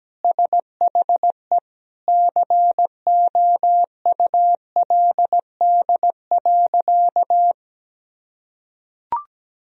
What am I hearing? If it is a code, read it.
Morse code, heard as a keyed tone, about 17 wpm, 700 Hz. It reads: SHE COULD.
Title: Sentences from Top 1000 Words - Review Part 3 - 17wpm